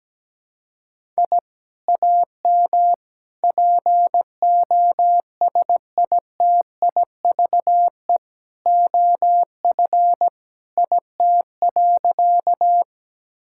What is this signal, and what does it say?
Morse code, heard as a keyed tone, about 17 wpm, 700 Hz. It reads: I AM POSITIVE OF IT.
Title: Sentences from Top 1000 Words - Review Part 5 - 17wpm